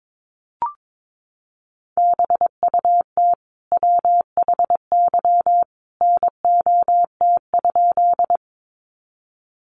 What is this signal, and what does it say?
Morse code, heard as a keyed tone, about 22 wpm, 700 Hz. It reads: BUT WHY NOT?
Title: Sentences from Top 600 Words 22wpm